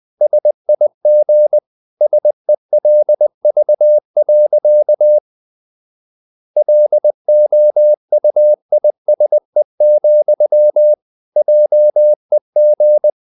Morse code, heard as a keyed tone, about 20 wpm, 600 Hz. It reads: SIG SELV. LOUISE, JEG
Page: Ferie 20 wpm